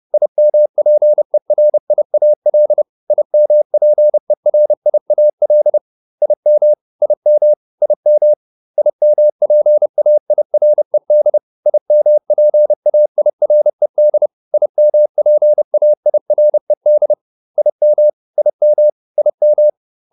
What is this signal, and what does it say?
Morse code, heard as a keyed tone, about 30 wpm, 600 Hz. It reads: IMPERIAL IMPERIAL IM IM IM IMPAIRED IMPAIRED IMPAIRED IM IM IM